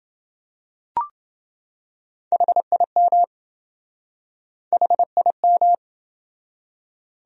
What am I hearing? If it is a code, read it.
Morse code, heard as a keyed tone, about 28 wpm, 700 Hz. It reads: HIM HIM